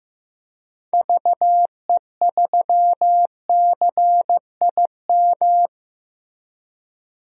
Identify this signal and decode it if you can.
Morse code, heard as a keyed tone, about 15 wpm, 700 Hz. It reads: VE3CIM
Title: DXCC-100 Call Signs - 6 Characters - Repeated 2x - 15wpm